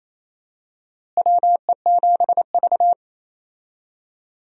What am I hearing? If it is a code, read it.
Morse code, heard as a keyed tone, about 28 wpm, 700 Hz. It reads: WE7V
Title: USA Call Signs - 4 Characters - Repeated 2x - 28wpm